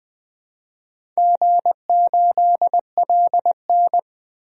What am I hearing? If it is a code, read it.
Morse code, heard as a keyed tone, about 20 wpm, 700 Hz. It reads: G8LN
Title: DXCC-100 Call Signs - 4 Characters - Repeated 2x - 20wpm